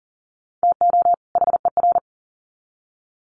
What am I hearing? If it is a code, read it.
Morse code, heard as a keyed tone, about 40 wpm, 700 Hz. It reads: TO HER
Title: Sets of 2 Words 40wpm